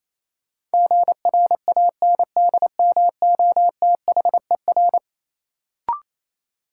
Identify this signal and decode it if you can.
Morse code, heard as a keyed tone, about 28 wpm, 700 Hz. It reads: GRANDMOTHER
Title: Sentences from Top 1200 Words - New Word Drill - 28wpm